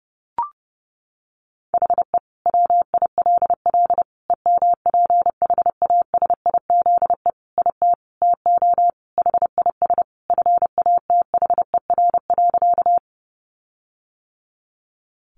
Morse code, heard as a keyed tone, about 30 wpm, 700 Hz. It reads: HE WILL EMPHASIZE IT TO HIS FATHER.
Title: Sentences from Top 1500 Words 30wpm